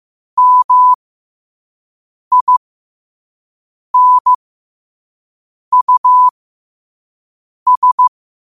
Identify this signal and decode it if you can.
Morse code, heard as a keyed tone, about 15 wpm, spaced as for 5 wpm, 1000 Hz. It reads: MINUS
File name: 5wpm